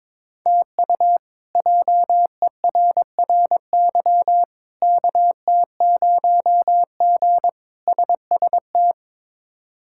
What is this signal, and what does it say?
Morse code, heard as a keyed tone, about 22 wpm, 700 Hz. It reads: TU JERRY KT0G SST
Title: K1USN SST QSO Practice - 22wpm